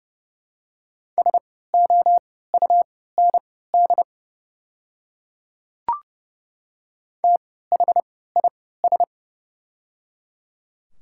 Morse code, heard as a keyed tone, about 30 wpm, spaced as for 17 wpm, 700 Hz. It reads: SOUND THIS
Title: 3x Character Spacing - Top 200 Words - 30wpm